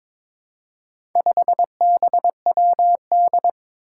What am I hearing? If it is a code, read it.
Morse code, heard as a keyed tone, about 21 wpm, 700 Hz. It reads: 5BWD
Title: Four Challenging Letter-Number Combinations 22wpm